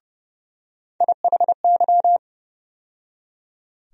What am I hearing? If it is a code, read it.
Morse code, heard as a keyed tone, about 30 wpm, 700 Hz. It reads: IHY
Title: USA Call Sign Postfixes - 3 Letters 30wpm